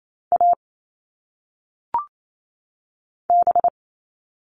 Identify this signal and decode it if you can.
Morse code, heard as a keyed tone, about 28 wpm, 700 Hz. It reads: A B
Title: All Cut Numbers - 1-Digit 28wpm